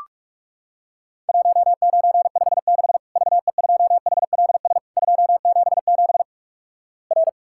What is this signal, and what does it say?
Morse code, heard as a keyed tone, about 46 wpm, 700 Hz. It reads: 1956 VE2HLS 277 R
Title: Worked All Europe DX Contest - QTC Exchange - 45wpm